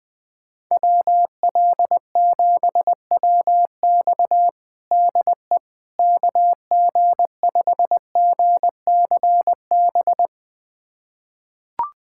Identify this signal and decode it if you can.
Morse code, heard as a keyed tone, about 20 wpm, 700 Hz. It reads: WL7WX DE KG5GCB
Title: USA Callsign DE Callsign 20wpm